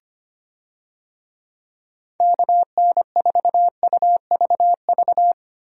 Morse code, heard as a keyed tone, about 25 wpm, 700 Hz. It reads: KN4UVV
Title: USA Call Signs - 6 Characters - Repeated 2x - 25wpm